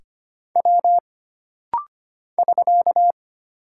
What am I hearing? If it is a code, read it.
Morse code, heard as a keyed tone, about 25 wpm, 700 Hz. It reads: W <SK>
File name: Single Character (Letter:Number:Punctuation:Prosign) - Rapid-Fire - 25wpm